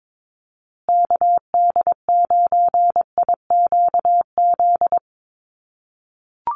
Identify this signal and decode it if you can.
Morse code, heard as a keyed tone, about 22 wpm, 700 Hz. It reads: KD9IQZ
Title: USA Call Signs Repeated 2x 22wpm